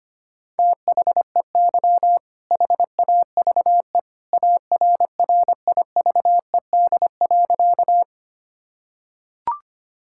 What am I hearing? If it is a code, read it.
Morse code, heard as a keyed tone, about 25 wpm, 700 Hz. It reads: THEY HAVE ARRIVED.
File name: practice / 25wpm / Sentences with 2-Word Phrases - 25wpm